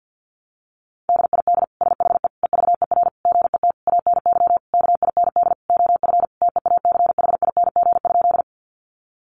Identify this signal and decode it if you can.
Morse code, heard as a keyed tone, about 48 wpm, 700 Hz. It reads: DID SHE EVER GET ANY KIND OF TEACHING?
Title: Sentences from Top 1100 Words 50wpm